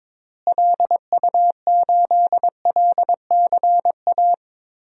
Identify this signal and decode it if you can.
Morse code, heard as a keyed tone, about 22 wpm, 700 Hz. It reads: LU8LCA
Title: DXCC-100 Call Signs - 6 Characters - Repeated 2x - 22wpm